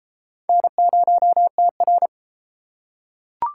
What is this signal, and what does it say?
Morse code, heard as a keyed tone, about 32 wpm, 700 Hz. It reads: N0TR